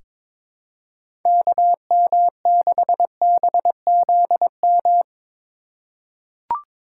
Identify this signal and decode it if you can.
Morse code, heard as a keyed tone, about 22 wpm, 700 Hz. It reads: KM6BZM
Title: USA Call Signs - 6 Characters - 3x Character Spacing - 22wpm